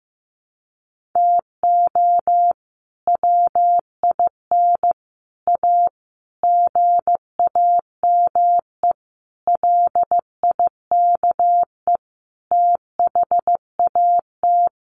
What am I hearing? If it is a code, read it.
Morse code, heard as a keyed tone, about 15 wpm, 700 Hz. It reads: TO WIN A GAME LIKE THAT
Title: Sentences from Top 1500 Words 15wpm